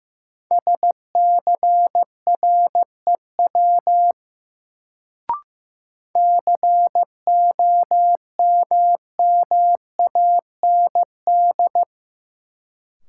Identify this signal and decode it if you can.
Morse code, heard as a keyed tone, about 16 wpm, 700 Hz. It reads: SCREW COMMAND
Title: Sentences from Top 1600 Words - New Word Drill Alt - 15wpm